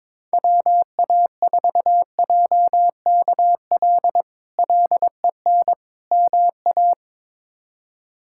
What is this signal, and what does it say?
Morse code, heard as a keyed tone, about 22 wpm, 700 Hz. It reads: WA4JKL LEN MA
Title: K1USN SST QSO Practice - 22wpm